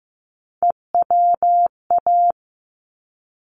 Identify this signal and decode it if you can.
Morse code, heard as a keyed tone, about 15 wpm, 700 Hz. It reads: EWA
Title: ARRL Sections 15wpm